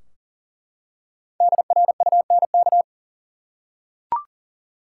Morse code, heard as a keyed tone, about 40 wpm, 700 Hz. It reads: DRUNK
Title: Five Letter Words - 40wpm